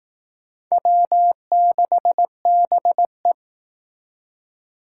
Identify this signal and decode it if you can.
Morse code, heard as a keyed tone, about 18 wpm, 700 Hz. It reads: W6BE
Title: Speed Racing - USA 1x2 Call Signs 15wpm